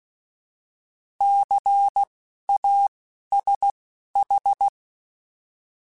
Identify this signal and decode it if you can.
Morse code, heard as a keyed tone, about 16 wpm, spaced as for 12 wpm, 800 Hz. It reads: CASH
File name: Alphabetical / Alphabetical mp3s / Cs @10